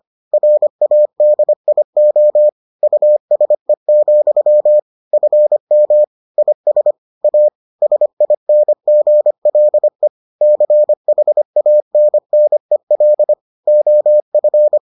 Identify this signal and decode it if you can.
Morse code, heard as a keyed tone, about 25 wpm, 600 Hz. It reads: RADIO USE, FM IS A SINGLE CHANNEL OF